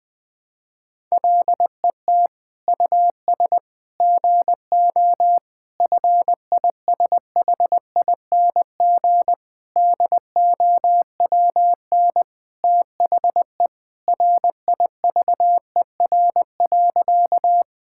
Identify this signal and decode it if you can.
Morse code, heard as a keyed tone, about 21 wpm, 700 Hz. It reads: LET US GO FISHING DOWN THE RIVER.
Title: Sentences from Top 1000 Words - Review Part 5 - 20wpm